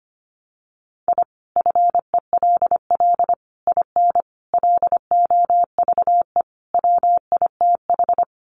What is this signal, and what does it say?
Morse code, heard as a keyed tone, about 25 wpm, 700 Hz. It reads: I FELL IN LOVE WITH
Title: Sets of 5 Words Encore 25wpm